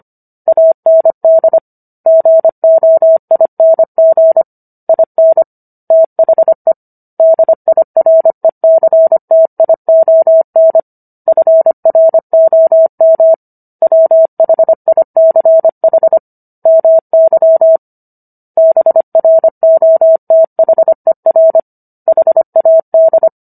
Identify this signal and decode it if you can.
Morse code, heard as a keyed tone, about 25 wpm, 650 Hz. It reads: AND GOING IN THE DIRECTION FROM WHICH MY BROTHER HAD
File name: Book One / 16-Chapter 16 - 25 WPM